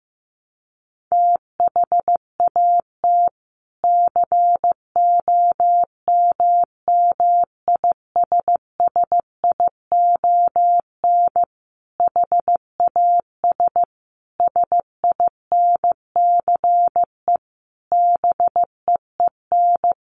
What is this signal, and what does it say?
Morse code, heard as a keyed tone, about 15 wpm, 700 Hz. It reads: THAT COMMISSION HAS SINCE BEEN
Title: Sentences from Top 1400 Words 15wpm